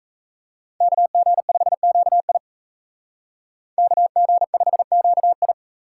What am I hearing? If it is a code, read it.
Morse code, heard as a keyed tone, about 36 wpm, 700 Hz. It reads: KG5QI KG5QI